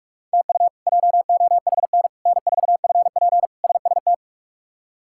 Morse code, heard as a keyed tone, about 41 wpm, 700 Hz. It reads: TU JOHN N4FP SST
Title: K1USN SST QSO Practice - 45wpm